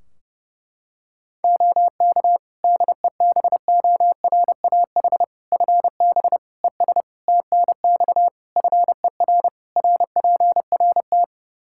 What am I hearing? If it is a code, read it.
Morse code, heard as a keyed tone, about 30 wpm, 700 Hz. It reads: OK DEBORAH FB ES TNX FER RPRT
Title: Practice QSO - Exchange 2 - Info - 30wpm